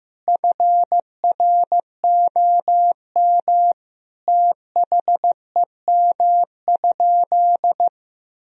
Morse code, heard as a keyed tone, about 15 wpm, 700 Hz. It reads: FROM THEM?
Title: Sentences from Top 1000 Words - Review Part 2 - 15wpm